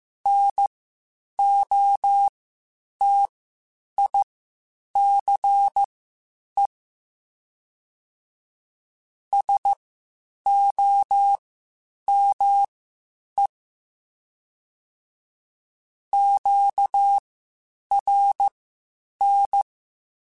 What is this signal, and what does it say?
Morse code, heard as a keyed tone, about 15 wpm, spaced as for 8 wpm, 800 Hz. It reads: NOTICE SOME QRN
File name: (spaced) / QSOs Spaced-mp3s / WD0BGP 1 @13 sp